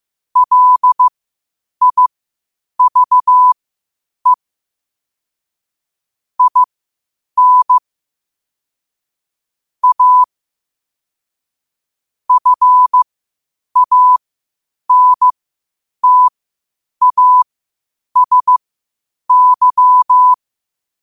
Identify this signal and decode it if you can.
Morse code, heard as a keyed tone, about 15 wpm, spaced as for 8 wpm, 1000 Hz. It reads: LIVE IN A FANTASY
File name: qotdfiles / 8wpm